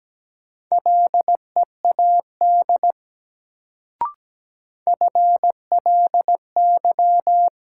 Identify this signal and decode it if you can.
Morse code, heard as a keyed tone, about 17 wpm, 700 Hz. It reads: LEAD FLY
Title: Sentences from Top 500 Words - New Word Drill - 17wpm